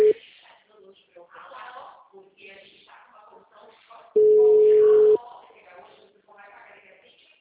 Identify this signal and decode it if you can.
Ringback tone (Europe, 425 Hz), with speech.